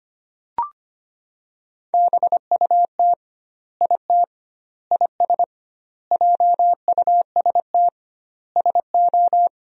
Morse code, heard as a keyed tone, about 25 wpm, 700 Hz. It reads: BUT IT IS JUST SO
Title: 2x Word-Spacing - Sentences from Top 200 Words - 25wpm